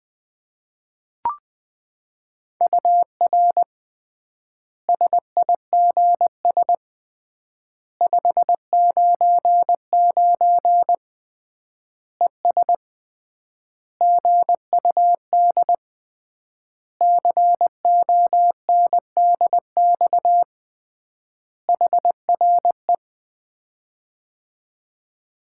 Morse code, heard as a keyed tone, about 20 wpm, 700 Hz. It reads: UR SIGS 599 ES GUD CONDX HRE
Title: Common QSO Phrases - 3x Word-Spacing - 20wpm